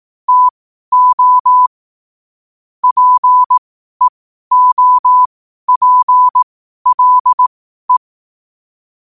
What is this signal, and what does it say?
Morse code, heard as a keyed tone, about 18 wpm, spaced as for 12 wpm, 1000 Hz. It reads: TO PEOPLE